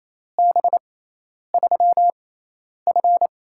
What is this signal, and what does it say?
Morse code, heard as a keyed tone, about 28 wpm, spaced as for 10 wpm, 700 Hz. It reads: B3F
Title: USA Call Signs - 4 Characters - 6x Character Spacing - 28wpm